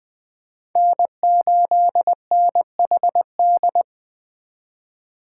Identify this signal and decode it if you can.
Morse code, heard as a keyed tone, about 20 wpm, 700 Hz. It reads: N8NHD